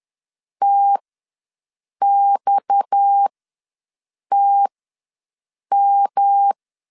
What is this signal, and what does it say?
Morse code, heard as a keyed tone, about 10 wpm, spaced as for 6 wpm, 800 Hz. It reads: TXTM
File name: LEZIONI / 6 PKX0 / 6 PKX ws 10wpm 3706